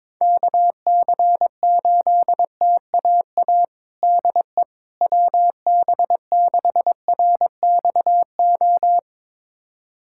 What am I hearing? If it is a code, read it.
Morse code, heard as a keyed tone, about 21 wpm, 700 Hz. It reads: KC8TAA DE WB6RXO
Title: USA Callsign DE Callsign - 6 Characters - 22wpm